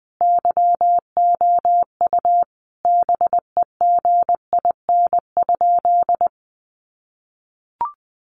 Morse code, heard as a keyed tone, about 20 wpm, 700 Hz. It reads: YOU BEGIN?
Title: Sentences from Top 300 Words 20wpm